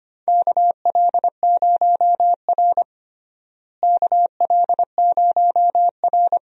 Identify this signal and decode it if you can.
Morse code, heard as a keyed tone, about 25 wpm, 700 Hz. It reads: KL0R KL0R
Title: USA Call Signs - 4 Characters - Repeated 2x - 25wpm